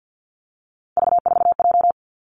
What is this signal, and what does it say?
Morse code, heard as a keyed tone, about 50 wpm, 700 Hz. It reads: V4J
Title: Three Challenging Letter-Number Combinations 50wpm